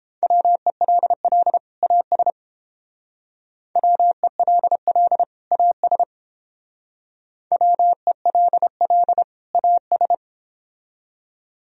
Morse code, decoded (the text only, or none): WELL AS WELL AS WELL AS